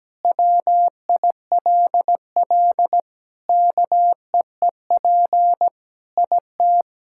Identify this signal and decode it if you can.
Morse code, heard as a keyed tone, about 17 wpm, 700 Hz. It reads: WILL KEEP IT
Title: Sentences from Top 1600 Words 17wpm